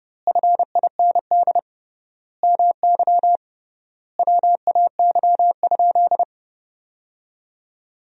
Morse code, heard as a keyed tone, about 30 wpm, 700 Hz. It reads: FIND MY WAY?
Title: 3x Word-Spacing - Sentences from Top 200 Words - 30wpm